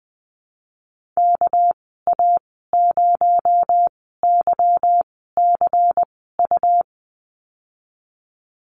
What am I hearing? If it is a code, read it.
Morse code, heard as a keyed tone, about 20 wpm, spaced as for 15 wpm, 700 Hz. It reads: KA0YCU
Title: 219 Lesson 35g - Number 0 - Call Signs 20wpm